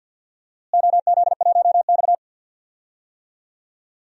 Morse code, heard as a keyed tone, about 50 wpm, 700 Hz. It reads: OZ1X
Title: DXCC-100 Call Signs - 4 Characters - Repeated 2x - 50wpm